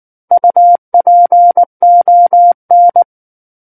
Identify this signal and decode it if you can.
Morse code, heard as a keyed tone, about 19 wpm, 700 Hz. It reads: UPON
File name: Book One / 10-Chapter 10 - 19 WPM